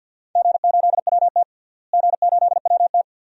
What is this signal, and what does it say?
Morse code, heard as a keyed tone, about 48 wpm, 700 Hz. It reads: G8WT G8WT
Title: DXCC-100 Call Signs - 4 Characters - Repeated 2x - 50wpm